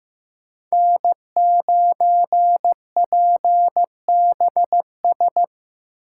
Morse code, heard as a keyed tone, about 16 wpm, 700 Hz. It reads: N9PBS
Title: USA Call Signs - 5 Characters - 3x Character Spacing - 15wpm